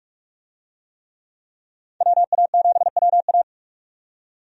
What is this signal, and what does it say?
Morse code, heard as a keyed tone, about 46 wpm, 700 Hz. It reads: WA7WA
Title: USA Call Signs - 5 Characters - 45wpm